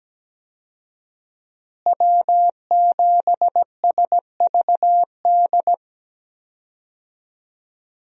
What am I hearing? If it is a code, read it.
Morse code, heard as a keyed tone, about 17 wpm, 700 Hz. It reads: W7SVD
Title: USA Call Signs - 5 Characters - Repeated 2x - 17wpm